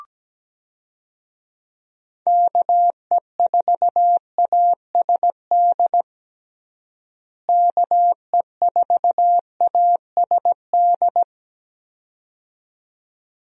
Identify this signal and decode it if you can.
Morse code, heard as a keyed tone, about 17 wpm, 700 Hz. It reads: KE4ASD KE4ASD
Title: USA Call Signs - 6 Characters - Repeated 2x - 17wpm